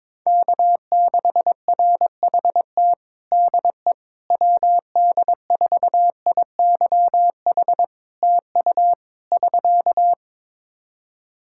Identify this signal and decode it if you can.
Morse code, heard as a keyed tone, about 22 wpm, 700 Hz. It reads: K6RHT DE WD4IYH TU <SK>
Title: Practice QSO - Exchange 4 - Ending - 22wpm